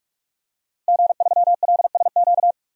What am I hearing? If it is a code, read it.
Morse code, heard as a keyed tone, about 48 wpm, 700 Hz. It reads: G3LSQ
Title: DXCC-100 Call Signs - 5 Characters - 45wpm